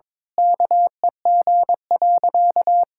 Morse code, heard as a keyed tone, about 22 wpm, 700 Hz. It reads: KEG.
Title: Pangram Challenge v2 - 22wpm